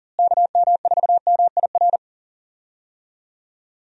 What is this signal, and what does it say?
Morse code, heard as a keyed tone, about 40 wpm, 700 Hz. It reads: KM4MIR
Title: USA Call Signs - 6 Characters - Repeated 2x - 40wpm